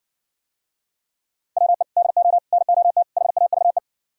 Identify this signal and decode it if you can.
Morse code, heard as a keyed tone, about 69 wpm, 700 Hz. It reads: WE DO NOT HAVE